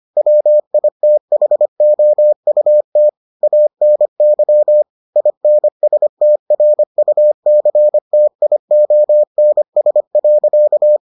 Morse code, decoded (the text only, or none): WITHOUT ANY INSTRUCTIONS.